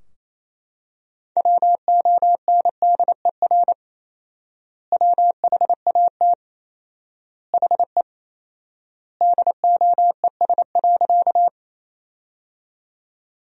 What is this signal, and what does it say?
Morse code, heard as a keyed tone, about 27 wpm, 700 Hz. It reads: WONDER WHAT HE DOES.
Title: 4x Word-Spacing - Sentences from Top 500 Words - 28wpm